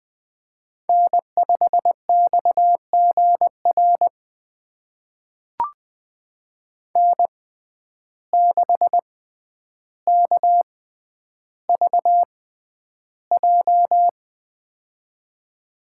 Morse code, heard as a keyed tone, about 20 wpm, spaced as for 10 wpm, 700 Hz. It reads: N5XGR N6KVJ
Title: USA Call Signs - 5 Characters - 6x Character Spacing - 20wpm